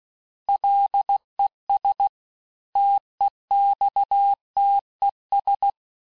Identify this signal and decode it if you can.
Morse code, heard as a keyed tone, about 16 wpm, 800 Hz. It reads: LES TEXTES
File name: cours-du-lundi files / cours-1592--teste-chiffre-vitesse-15mpm